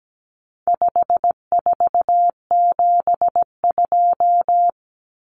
Morse code, heard as a keyed tone, about 17 wpm, 700 Hz. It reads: 5472
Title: Common Cut Numbers - 4-Digit 17wpm